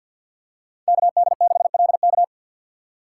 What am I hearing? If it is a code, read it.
Morse code, heard as a keyed tone, about 47 wpm, 700 Hz. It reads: KD6LK
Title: USA Call Signs - 5 Characters - 2x Character Spacing - 50wpm